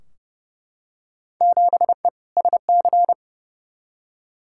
Morse code, heard as a keyed tone, about 31 wpm, 700 Hz. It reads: 7E SC
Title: Field Day Exchanges Encore - 30wpm